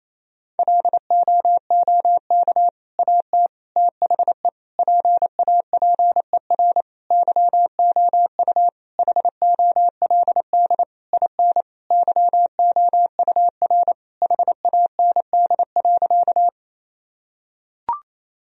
Morse code, heard as a keyed tone, about 28 wpm, 700 Hz. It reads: LOOK AT THE PAPER YOU HOLD IN YOUR HAND.
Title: Sentences from Top 1000 Words - Review Part 2 - 28wpm